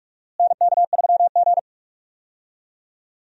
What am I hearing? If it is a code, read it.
Morse code, heard as a keyed tone, about 44 wpm, 700 Hz. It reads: NK3G
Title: USA Call Signs - 4 Characters - Repeated 2x - 45wpm